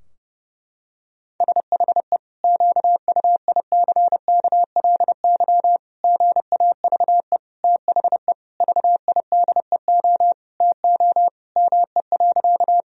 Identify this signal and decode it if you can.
Morse code, heard as a keyed tone, about 30 wpm, 700 Hz. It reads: SHE QUICKLY GAVE THE VIDEO TO ME.
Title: Sentences from Top 1000 Words - Review Part 4 - 30wpm